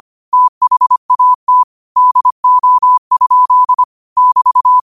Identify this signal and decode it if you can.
Morse code, heard as a keyed tone, about 25 wpm, 1000 Hz. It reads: THAT DO? <BT>
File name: qotdfiles / 25wpm